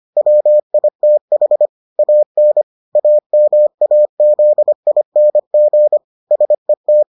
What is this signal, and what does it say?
Morse code, heard as a keyed tone, about 25 wpm, 600 Hz. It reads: WITH AN AMAZING SET